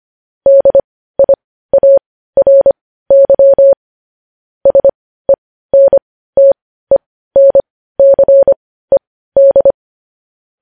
Morse code, heard as a keyed tone, about 25 wpm, spaced as for 15 wpm, 550 Hz. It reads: DIARY SENTENCED